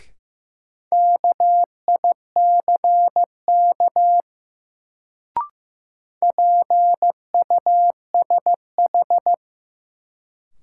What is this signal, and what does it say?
Morse code, heard as a keyed tone, about 15 wpm, 700 Hz. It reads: KICK PUSH
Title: Sentences from Top 1300 Words - New Word Drill Alt - 15wpm